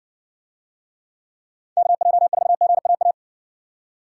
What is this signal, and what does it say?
Morse code, heard as a keyed tone, about 58 wpm, 700 Hz. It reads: KJ4GAA